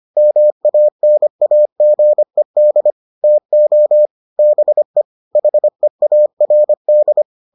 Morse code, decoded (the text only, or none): MANAGED TO BE HEARD